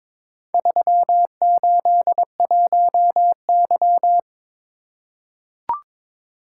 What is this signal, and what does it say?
Morse code, heard as a keyed tone, about 22 wpm, 700 Hz. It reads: 381Y
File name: practice / 22wpm / Four Letter-Number Combinations - 22wpm